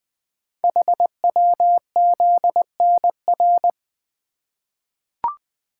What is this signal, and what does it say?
Morse code, heard as a keyed tone, about 20 wpm, 700 Hz. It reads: HWZNR
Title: Code Groups - 5 Letter:Numbers - 20wpm